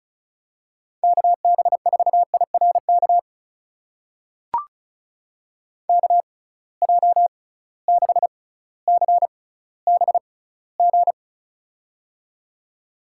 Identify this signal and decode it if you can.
Morse code, heard as a keyed tone, about 36 wpm, spaced as for 18 wpm, 700 Hz. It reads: KB4IRK KJ6CBG
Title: USA Call Signs - 6 Characters - 6x Character Spacing - 35wpm